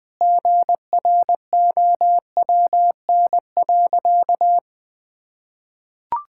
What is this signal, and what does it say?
Morse code, heard as a keyed tone, about 20 wpm, 700 Hz. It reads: GROWN.
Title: Sentences with 5-Word Phrases - 20wpm